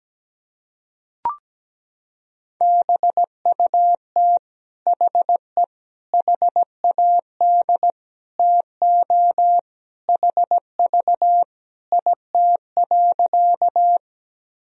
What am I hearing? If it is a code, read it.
Morse code, heard as a keyed tone, about 17 wpm, 700 Hz. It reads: BUT HE HAD TO HV IT.
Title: Sentences from Top 100 Words and Abbreviations 17wpm